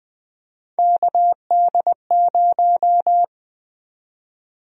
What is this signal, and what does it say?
Morse code, heard as a keyed tone, about 20 wpm, 700 Hz. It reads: KD0
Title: USA Call Sign Prefixes - 20wpm